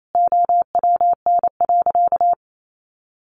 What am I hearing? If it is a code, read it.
Morse code, heard as a keyed tone, about 28 wpm, 700 Hz. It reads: OWN.
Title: Sentences from Top 1200 Words 28wpm